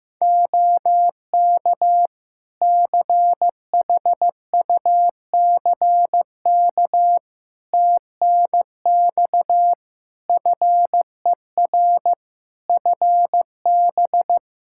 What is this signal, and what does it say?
Morse code, heard as a keyed tone, about 15 wpm, 700 Hz. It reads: OK CHUCK TNX FER FB
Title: Practice QSO - Exchange 4 - Ending - 15wpm